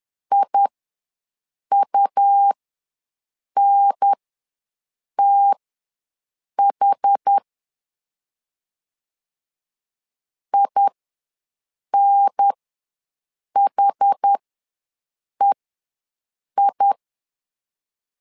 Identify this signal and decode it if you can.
Morse code, heard as a keyed tone, about 10 wpm, spaced as for 6 wpm, 800 Hz. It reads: IUNTH INHEI
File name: LEZIONI / 2 UHDN / 2 UHDN ws 10wpm 2044